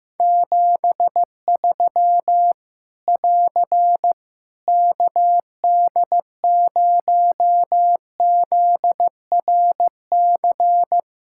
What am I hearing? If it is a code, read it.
Morse code, heard as a keyed tone, about 15 wpm, 700 Hz. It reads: 73 <AR> KD0ZRC
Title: Practice QSO - Exchange 4 - Ending - 15wpm